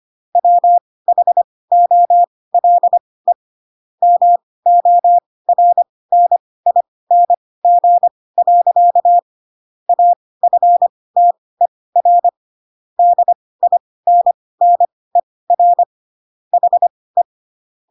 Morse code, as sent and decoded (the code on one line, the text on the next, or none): .-- .... --- .-.. / -- --- .-. -. .. -. --. .-.-.- / .- ..-. - . .-. / -.. .. -. -. .-. / .... .
WHOLE MORNING. AFTER DINNER HE